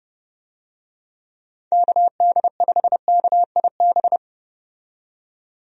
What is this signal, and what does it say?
Morse code, heard as a keyed tone, about 30 wpm, 700 Hz. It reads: KD5KIB
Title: USA Call Signs - 6 Characters - 30wpm